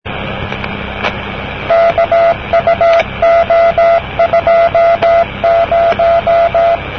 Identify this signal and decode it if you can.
Morse code, heard as a keyed tone, about 17 wpm, 680 Hz, with noise. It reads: KUO20
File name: namerica / usa / usa-usad-cw-kuo20-de-kkn50-rb-1973